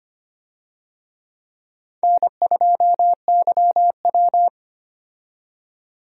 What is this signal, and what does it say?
Morse code, heard as a keyed tone, about 25 wpm, 700 Hz. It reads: N2YW
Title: USA Call Signs - 4 Characters - 25wpm